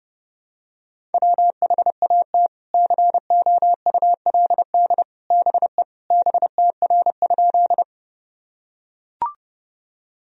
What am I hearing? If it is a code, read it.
Morse code, heard as a keyed tone, about 30 wpm, 700 Hz. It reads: WHAT COULD BE BTR?
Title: Sentences from Top 400 Words and Abbreviations 30wpm